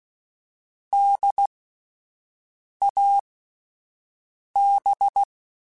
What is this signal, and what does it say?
Morse code, heard as a keyed tone, about 16 wpm, spaced as for 5 wpm, 800 Hz. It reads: DAB